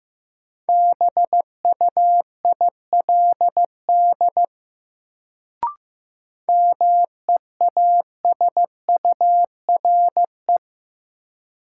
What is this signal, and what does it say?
Morse code, heard as a keyed tone, about 15 wpm, 700 Hz. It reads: BUILD MEASURE
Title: Sentences from Top 1000 Words - Review Part 1 - New Word Drill - 15wpm